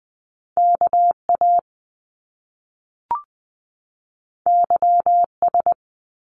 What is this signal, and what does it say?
Morse code, heard as a keyed tone, about 20 wpm, 700 Hz. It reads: KA YS
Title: Top Two Letter Combinations in Words 20wpm